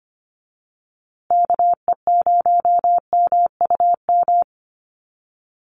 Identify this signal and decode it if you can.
Morse code, heard as a keyed tone, about 25 wpm, 700 Hz. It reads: KE0MUM
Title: USA Call Signs Repeated 3x 25wpm